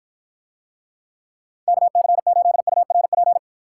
Morse code, heard as a keyed tone, about 49 wpm, 700 Hz. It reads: KC8URP